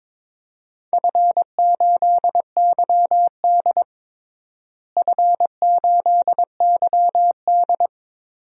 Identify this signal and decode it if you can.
Morse code, heard as a keyed tone, about 22 wpm, 700 Hz. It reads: F8YD F8YD
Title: DXCC-100 Call Signs - 4 Characters - Repeated 2x - 22wpm